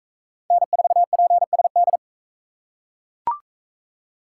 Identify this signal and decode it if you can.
Morse code, heard as a keyed tone, about 39 wpm, 700 Hz. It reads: N4PSD